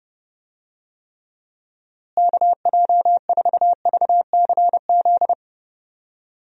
Morse code, heard as a keyed tone, about 30 wpm, 700 Hz. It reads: KJ4VCZ
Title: USA Call Signs - 6 Characters - Repeated 2x - 30wpm